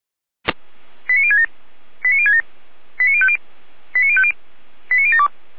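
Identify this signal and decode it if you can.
Five-tone selective call (ZVEI), heard as quick runs of five tones, tones of about 80 ms. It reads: 89067 89067 89050 89050 89072
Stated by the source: ZVEI five-tone call received by SDR.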